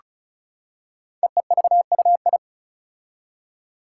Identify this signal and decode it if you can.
Morse code, heard as a keyed tone, about 36 wpm, 690 Hz. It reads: EEVUI
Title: Code Groups - 5 Letters - 35wpm